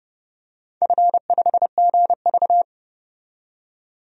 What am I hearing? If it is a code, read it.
Morse code, heard as a keyed tone, about 30 wpm, 700 Hz. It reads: F5GV
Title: DXCC-100 Call Signs - 4 Characters - 30wpm